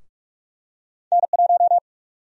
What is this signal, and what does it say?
Morse code, heard as a keyed tone, about 46 wpm, 700 Hz. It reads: N1
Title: USA Call Sign Prefixes - Repeated 2x - 45wpm